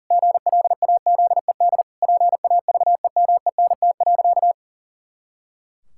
Morse code, heard as a keyed tone, about 40 wpm, 700 Hz. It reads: GLAZED PAVEMENT.
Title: Pangram Challenge v2 - 40wpm